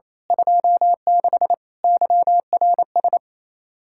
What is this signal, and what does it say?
Morse code, heard as a keyed tone, about 29 wpm, 700 Hz. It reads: 26 YRS
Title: Speed-Racing Common QSO Phrases - 28wpm